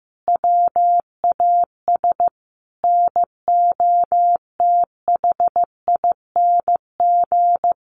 Morse code, heard as a keyed tone, about 15 wpm, 700 Hz. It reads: WAS NOTHING
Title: Sentences from Top 1000 Words 15wpm